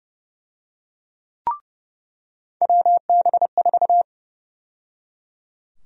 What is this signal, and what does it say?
Morse code, heard as a keyed tone, about 30 wpm, 700 Hz. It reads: WB4
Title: USA Call Sign Prefixes - 30wpm